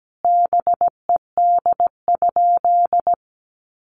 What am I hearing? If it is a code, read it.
Morse code, heard as a keyed tone, about 17 wpm, 700 Hz. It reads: BED?
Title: Sentences from Top 1200 Words 17wpm